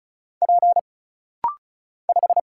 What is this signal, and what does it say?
Morse code, heard as a keyed tone, about 36 wpm, 700 Hz. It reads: P 5
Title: Single Letter-Number Alt - Rapid-Fire 35wpm